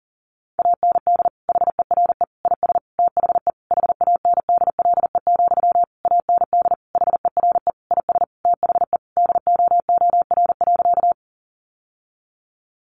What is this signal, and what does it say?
Morse code, heard as a keyed tone, about 40 wpm, 700 Hz. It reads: AND HERE IS THE HANDLE, AND HERE IS THE DOOR.
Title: Sentences from Top 1100 Words 40wpm